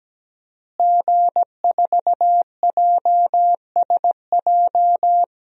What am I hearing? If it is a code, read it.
Morse code, heard as a keyed tone, about 17 wpm, 700 Hz. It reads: G4JSJ
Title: DXCC-100 Call Signs - 5 Characters - Repeated 2x - 17wpm